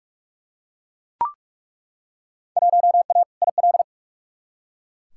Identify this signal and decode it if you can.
Morse code, heard as a keyed tone, about 43 wpm, 700 Hz. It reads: 1A IL
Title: Field Day Exchanges Encore - 45wpm